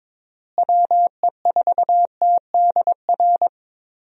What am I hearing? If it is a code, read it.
Morse code, heard as a keyed tone, about 22 wpm, 700 Hz. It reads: WE4TDR
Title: USA Call Signs - 6 Characters - Repeated 3x - 22wpm